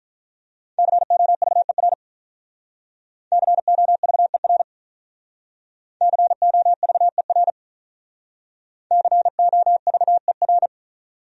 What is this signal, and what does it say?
Morse code, heard as a keyed tone, about 47 wpm, 700 Hz. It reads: COVER COVER COVER COVER